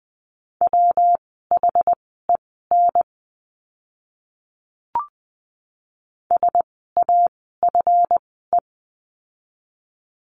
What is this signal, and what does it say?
Morse code, heard as a keyed tone, about 20 wpm, spaced as for 14 wpm, 700 Hz. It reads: WHEN SAFE
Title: 139 Lesson 23f - Letter f - Words 20wpm